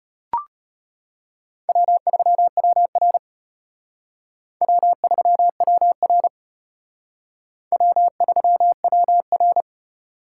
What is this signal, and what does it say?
Morse code, heard as a keyed tone, about 30 wpm, 700 Hz. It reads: W3WR W3WR W3WR
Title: Speed Racing - USA 1x2 Call Signs 25wpm